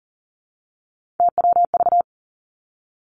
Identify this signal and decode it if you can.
Morse code, heard as a keyed tone, about 40 wpm, 700 Hz. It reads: TWV
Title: Three Challenging Letter-Number Combinations 40wpm